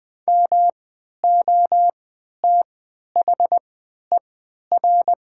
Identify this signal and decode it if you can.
Morse code, heard as a keyed tone, about 20 wpm, spaced as for 11 wpm, 700 Hz. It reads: MOTHER
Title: 3x Character Spacing - Top 200 Words - 20wpm